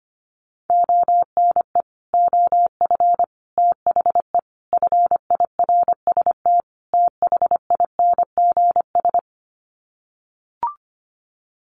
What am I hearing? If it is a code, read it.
Morse code, heard as a keyed tone, about 25 wpm, 700 Hz. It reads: ONE OF THE FIRST THINGS
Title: Sets of 5 Words Encore 25wpm